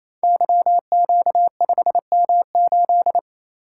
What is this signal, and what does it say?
Morse code, heard as a keyed tone, about 29 wpm, 700 Hz. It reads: YQ5M8
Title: Code Groups - 5 Letter:Numbers - 28wpm